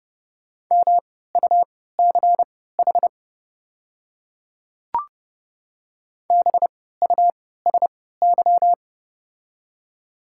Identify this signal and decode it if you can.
Morse code, heard as a keyed tone, about 30 wpm, spaced as for 17 wpm, 690 Hz. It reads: MUCH BUSY
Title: 3x Character Spacing - Top 500 Words - 30wpm